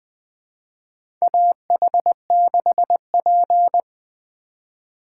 Morse code, heard as a keyed tone, about 20 wpm, 700 Hz. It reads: AH6P
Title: USA Call Signs - 4 Characters - 4x Character Spacing - 20wpm